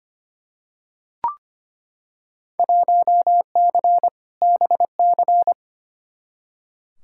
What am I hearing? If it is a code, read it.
Morse code, heard as a keyed tone, about 25 wpm, 700 Hz. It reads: 1C BC
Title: Field Day Exchanges Encore - 25wpm